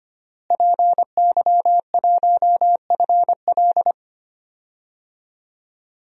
Morse code, heard as a keyed tone, about 25 wpm, 700 Hz. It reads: PY1FL
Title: DXCC-100 Call Signs - 5 Characters - Repeated 2x - 25wpm